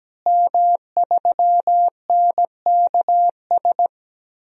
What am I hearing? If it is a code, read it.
Morse code, heard as a keyed tone, about 17 wpm, 700 Hz. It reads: M3NKS
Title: DXCC-100 Call Signs - 5 Characters - Repeated 3x - 17wpm